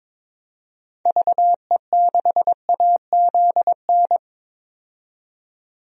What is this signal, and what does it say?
Morse code, heard as a keyed tone, about 22 wpm, 700 Hz. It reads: VE6AZN
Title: DXCC-100 Call Signs - 6 Characters - 22wpm